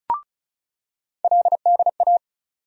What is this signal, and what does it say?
Morse code, heard as a keyed tone, about 34 wpm, 700 Hz. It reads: LDA
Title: USA Call Sign Postfixes - 3 Letters 35wpm